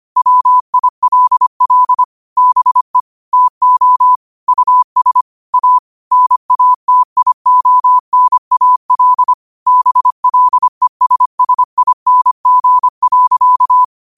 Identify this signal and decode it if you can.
Morse code, heard as a keyed tone, about 25 wpm, 1000 Hz. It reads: WILL BE TO US A NATIONAL BLESSING.
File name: qotdfiles / 25wpm